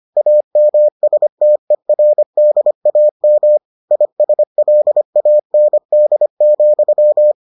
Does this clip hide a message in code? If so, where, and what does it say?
Yes: Morse code from 0.2 to 7.3 s, heard as a keyed tone, about 25 wpm, 600 Hz. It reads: AMSTERDAM ISLAND,